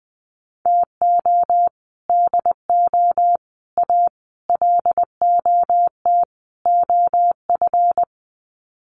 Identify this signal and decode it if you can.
Morse code, heard as a keyed tone, about 20 wpm, 700 Hz. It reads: TO DO A LOT OF
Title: Sets of 5 Words Encore 20wpm